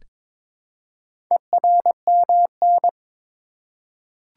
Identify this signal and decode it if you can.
Morse code, heard as a keyed tone, about 22 wpm, 700 Hz. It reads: ERMN
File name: practice / 22wpm / Four Letter-Number Combinations - 22wpm